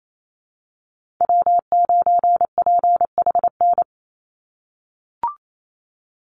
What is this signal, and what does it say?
Morse code, heard as a keyed tone, about 28 wpm, 700 Hz. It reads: W9PHN
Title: USA Call Signs 28wpm